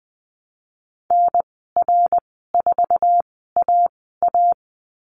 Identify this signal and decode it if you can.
Morse code, heard as a keyed tone, about 20 wpm, spaced as for 14 wpm, 700 Hz. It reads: NR4AA
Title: 062 Lesson 11g - Letter h - Call Signs 20wpm